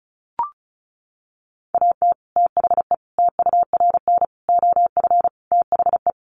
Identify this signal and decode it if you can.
Morse code, heard as a keyed tone, about 34 wpm, 700 Hz. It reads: AT THE TURN OF THE
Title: Sets of 5 Words 35wpm